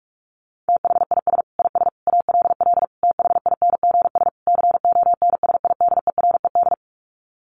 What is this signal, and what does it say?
Morse code, heard as a keyed tone, about 44 wpm, 700 Hz. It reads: THIS IS ALL THINGS CONSIDERED
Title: Sets of 5 Words 45wpm